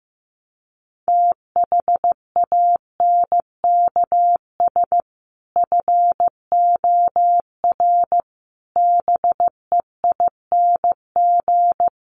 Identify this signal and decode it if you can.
Morse code, heard as a keyed tone, about 15 wpm, 700 Hz. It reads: THANKS FOR BEING
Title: Sets of 5 Words 15wpm